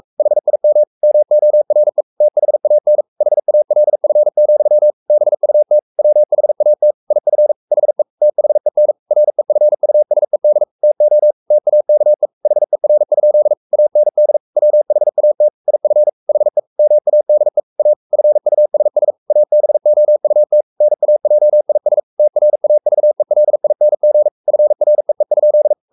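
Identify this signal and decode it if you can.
Morse code, heard as a keyed tone, about 44 wpm, 600 Hz. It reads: HIM MORE THAN HALF, BUT WHAT IF HE THEN REFUSED TO TAKE HER? AND WHAT IF HE MADE A FUSS ABOUT NAJIS TRAVELING FREE?
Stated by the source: Morse code at 43 wpm sending Ken Follett's novel Never in English, from CW med Gnister.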